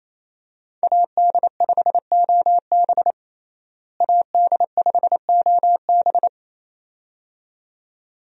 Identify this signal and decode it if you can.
Morse code, heard as a keyed tone, about 28 wpm, 700 Hz. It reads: AD5OB AD5OB
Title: USA Call Signs - 5 Characters - Repeated 2x - 28wpm